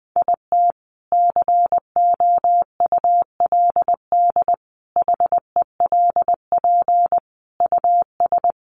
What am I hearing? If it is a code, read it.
Morse code, heard as a keyed tone, about 20 wpm, 700 Hz. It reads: IT COULD HELP US
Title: Sentences from Top 1400 Words 20wpm